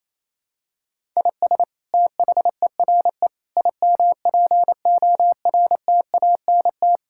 Morse code, heard as a keyed tone, about 28 wpm, 700 Hz. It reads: IS THERE IMPORTANT
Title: Sentences from Top 1000 Words - Review Part 6 - 28wpm